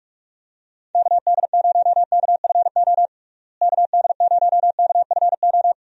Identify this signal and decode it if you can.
Morse code, heard as a keyed tone, about 46 wpm, 700 Hz. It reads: KD0KFO KD0KFO
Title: USA Call Signs - 6 Characters - Repeated 3x - 45wpm